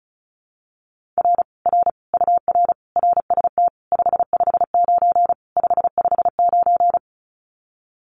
Morse code, heard as a keyed tone, about 36 wpm, 700 Hz. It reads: R R UR RST 559 559
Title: Common QSO Phrases Encore 35wpm